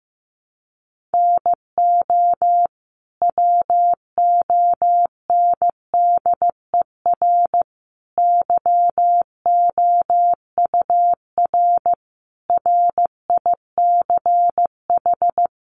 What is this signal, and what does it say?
Morse code, heard as a keyed tone, about 14 wpm, 700 Hz. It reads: NO WONDER YOUR RICH
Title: Sentences from Top 1100 Words 15wpm